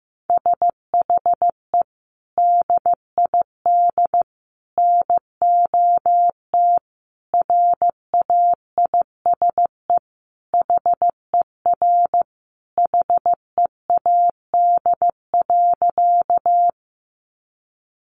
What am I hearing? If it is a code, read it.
Morse code, heard as a keyed tone, about 15 wpm, 700 Hz. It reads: SHE DID NOT RAISE HER HEAD.
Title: Sentences from Top 1400 Words 15wpm